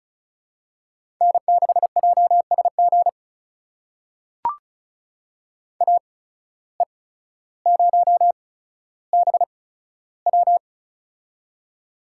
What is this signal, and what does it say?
Morse code, heard as a keyed tone, about 36 wpm, spaced as for 15 wpm, 700 Hz. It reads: N6JSG AE0BW
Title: USA Call Signs - 5 Characters - 8x Character Spacing - 35wpm